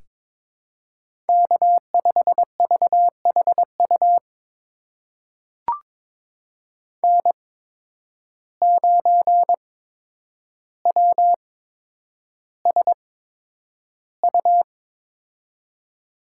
Morse code, heard as a keyed tone, about 22 wpm, spaced as for 9 wpm, 700 Hz. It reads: K5VHU N9WSU
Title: USA Call Signs - 8x Character Spacing - 22wpm